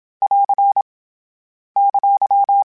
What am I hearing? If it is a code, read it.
Morse code, heard as a keyed tone, about 27 wpm, 800 Hz. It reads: <AR> !
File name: General / General-mp3s / ESE ING @25